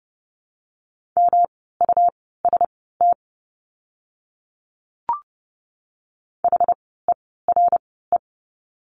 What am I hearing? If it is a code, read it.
Morse code, heard as a keyed tone, about 30 wpm, spaced as for 17 wpm, 700 Hz. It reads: MUST HERE
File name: course / 30:10wpm / 237 Lesson 38f - Letter q - Words 30wpm